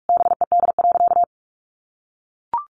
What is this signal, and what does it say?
Morse code, heard as a keyed tone, about 44 wpm, 700 Hz. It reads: BED.
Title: Sentences from Top 900 Words 45wpm